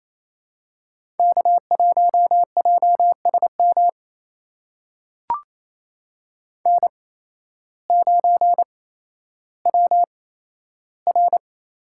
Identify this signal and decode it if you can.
Morse code, heard as a keyed tone, about 28 wpm, spaced as for 13 wpm, 700 Hz. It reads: K1JSM N9WR
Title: USA Call Signs - 5 Characters - 8x Character Spacing - 28wpm